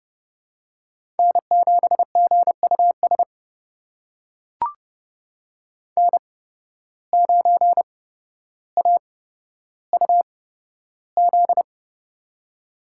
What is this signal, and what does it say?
Morse code, heard as a keyed tone, about 30 wpm, spaced as for 13 wpm, 700 Hz. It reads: N7GUS N9AUZ
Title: USA Call Signs - 5 Characters - 8x Character Spacing - 30wpm